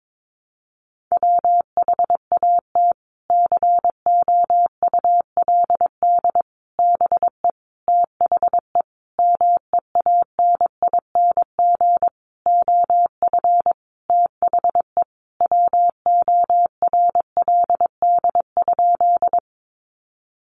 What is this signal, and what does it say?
Morse code, heard as a keyed tone, about 22 wpm, 700 Hz. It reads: WHAT COULD BE THE MEANING OF THE WORLD?
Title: Sentences from Top 900 Words 22wpm